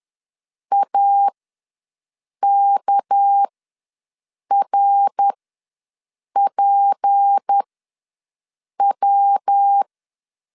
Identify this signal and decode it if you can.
Morse code, heard as a keyed tone, about 10 wpm, spaced as for 6 wpm, 800 Hz. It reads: AKRPW
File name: LEZIONI / 8 YJ45 / 8 YJ ws 10wpm 4043